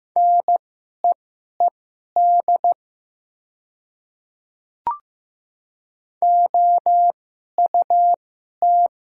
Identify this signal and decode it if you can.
Morse code, heard as a keyed tone, about 15 wpm, spaced as for 11 wpm, 700 Hz. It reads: NEED OUT
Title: 2x Character Spacing - Top 200 Words - 15wpm